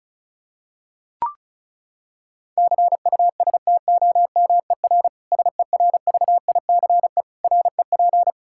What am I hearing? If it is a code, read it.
Morse code, heard as a keyed tone, about 34 wpm, 700 Hz. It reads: CUSTOMER SERVICE REP
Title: Practice QSO - Exchange 3 - More Info - 35wpm